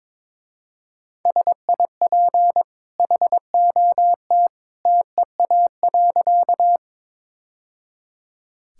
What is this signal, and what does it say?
Morse code, heard as a keyed tone, about 22 wpm, 690 Hz. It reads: SIP HOT TEA.
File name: practice / 22wpm / Easy Sentences with Hints - 22wpm